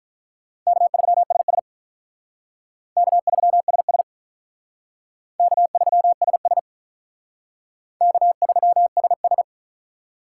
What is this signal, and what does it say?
Morse code, heard as a keyed tone, about 43 wpm, 700 Hz. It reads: K3SS K3SS K3SS K3SS
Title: Speed Racing - USA 1x2 Call Signs 35wpm